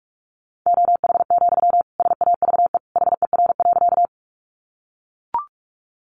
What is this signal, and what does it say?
Morse code, heard as a keyed tone, about 44 wpm, 700 Hz. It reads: OH, SAVE HER.
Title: Sentences from Top 1100 Words 45wpm